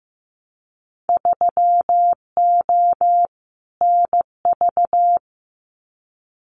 Morse code, heard as a keyed tone, about 15 wpm, 700 Hz. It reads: 3O NV
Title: Winter Field Day Exchanges Encore 15wpm